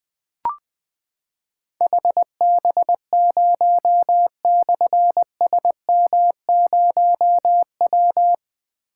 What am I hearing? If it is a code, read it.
Morse code, heard as a keyed tone, about 20 wpm, 700 Hz. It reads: HB0/SM0W
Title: Difficult Call Signs 20wpm